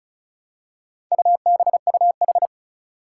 Morse code, heard as a keyed tone, about 36 wpm, 700 Hz. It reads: UBUH
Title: Four Challenging Letter-Number Combinations 35wpm